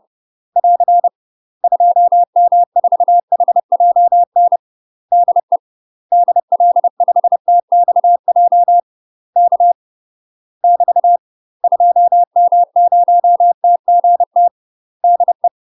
Morse code, heard as a keyed tone, about 30 wpm, 700 Hz. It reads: <AR> 2M4HJN DE DL5TXJ K <BT> 2M0TGT DE
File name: UK MASTER QSOs TRAINING / 30WPM / UK MASTER QSOs 0039